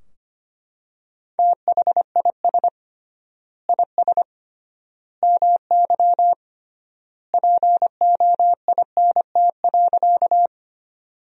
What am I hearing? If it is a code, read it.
Morse code, heard as a keyed tone, about 25 wpm, 700 Hz. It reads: THIS IS MY POINT.
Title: 3x Word-Spacing - Sentences from Top 200 Words - 25wpm